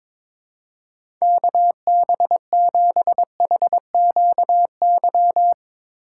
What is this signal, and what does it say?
Morse code, heard as a keyed tone, about 22 wpm, 700 Hz. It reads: KB7HQY
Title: USA Call Signs - 2x Character Spacing - 22wpm